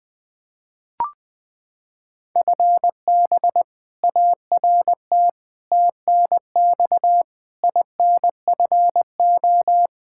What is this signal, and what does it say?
Morse code, heard as a keyed tone, about 20 wpm, 700 Hz. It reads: FB ART TNX INFO
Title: K1USN SST QSO Practice 20wpm